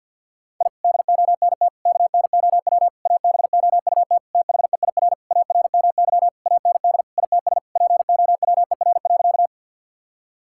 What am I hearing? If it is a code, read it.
Morse code, heard as a keyed tone, about 53 wpm, 690 Hz. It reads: I DONT KNOW ABOUT THEIR ARMY AND ITS POWER.
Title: Sentences from Top 1000 Words - Review Part 5 - 50wpm